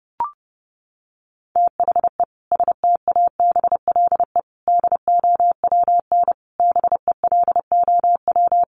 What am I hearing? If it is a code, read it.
Morse code, heard as a keyed tone, about 30 wpm, 700 Hz. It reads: THE STABLE DOWN BELOW
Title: Sentences from Top 1500 Words 30wpm